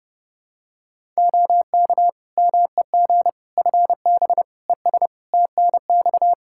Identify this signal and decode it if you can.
Morse code, heard as a keyed tone, about 32 wpm, 700 Hz. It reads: OK MEG FB ES TNX
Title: Practice QSO - Exchange 2 - Info - 30wpm